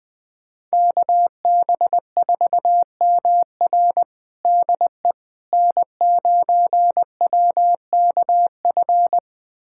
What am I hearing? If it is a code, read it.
Morse code, heard as a keyed tone, about 21 wpm, 700 Hz. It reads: KB4MR DE N9WKF
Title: USA Callsign DE Callsign - 5 Characters - 20wpm